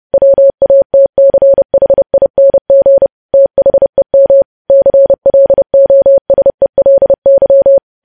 Morse code, heard as a keyed tone, about 30 wpm, 550 Hz. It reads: WATCHING THEM CLOSELY
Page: news headlines in morse code at 30 wpm